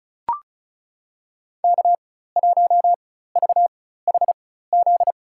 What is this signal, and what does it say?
Morse code, heard as a keyed tone, about 36 wpm, spaced as for 16 wpm, 700 Hz. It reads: K1VHZ
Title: USA Call Signs - 5 Characters - 4x Character Spacing - 35wpm